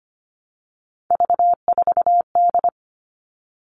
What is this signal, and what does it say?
Morse code, heard as a keyed tone, about 25 wpm, 700 Hz. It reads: V4D
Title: All Cut Numbers - 3-Digit 25wpm